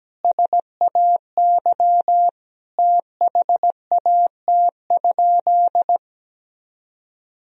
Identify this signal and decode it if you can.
Morse code, heard as a keyed tone, about 17 wpm, 700 Hz. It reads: SAY THAT?
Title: Sentences from Top 200 Words and Abbreviations 17wpm